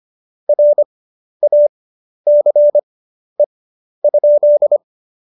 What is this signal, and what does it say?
Morse code, heard as a keyed tone, about 25 wpm, spaced as for 11 wpm, 590 Hz. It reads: RACE?
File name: Jokes-25-char-10-eff